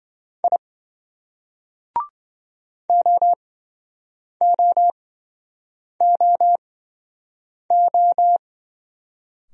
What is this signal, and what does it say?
Morse code, heard as a keyed tone, about 25 wpm, spaced as for 7 wpm, 700 Hz. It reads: I OOOO